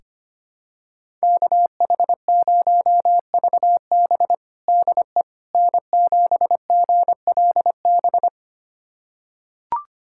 Morse code, heard as a keyed tone, about 25 wpm, 700 Hz. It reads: KH0VB DE N7GLB
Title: USA Callsign DE Callsign - 5 Characters - 25wpm